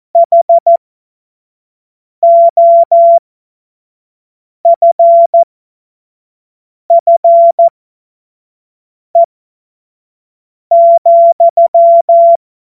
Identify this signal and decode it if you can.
Morse code, heard as a keyed tone, about 14 wpm, spaced as for 5 wpm, 680 Hz. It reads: HOFFE,
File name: Morsekurs / Witze / Witz 04. Januar 2022, 25 F70